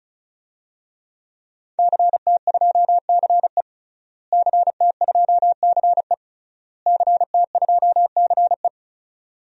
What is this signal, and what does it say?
Morse code, heard as a keyed tone, about 34 wpm, 700 Hz. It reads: CT2CE CT2CE CT2CE
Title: DXCC-100 Call Signs - 5 Characters - Repeated 3x - 35wpm